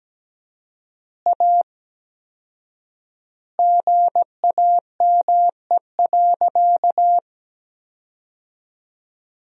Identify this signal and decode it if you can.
Morse code, heard as a keyed tone, about 17 wpm, 700 Hz. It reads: A GAME.
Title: 4x Word-Spacing - Sentences from Top 500 Words - 17wpm